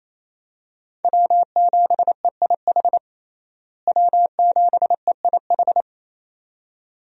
Morse code, heard as a keyed tone, about 28 wpm, 700 Hz. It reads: W7EIH W7EIH
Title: USA Call Signs - 5 Characters - Repeated 2x - 28wpm